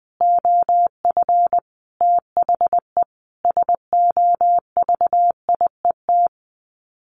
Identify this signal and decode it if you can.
Morse code, heard as a keyed tone, about 19 wpm, 700 Hz. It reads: OF THE SOVIET
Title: Sets of 5 Words Encore 20wpm